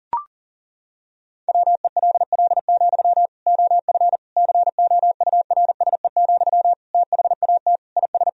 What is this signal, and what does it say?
Morse code, heard as a keyed tone, about 40 wpm, 700 Hz. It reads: WELL, OF COURSE, THAT IS
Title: Sentences from Top 1000 Words - Review Part 1 - 40wpm